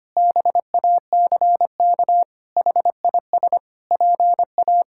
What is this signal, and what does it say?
Morse code, heard as a keyed tone, about 25 wpm, 700 Hz. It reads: BACK HIS PA
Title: Pangram Challenge v2 - 25wpm